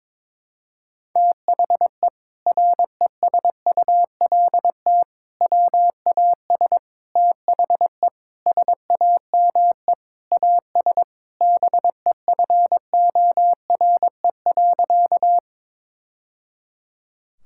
Morse code, heard as a keyed tone, about 22 wpm, 700 Hz. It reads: THE RESULT WAS THE SAME AS BEFORE.
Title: Sentences from Top 1000 Words - Review Part 5 - 22wpm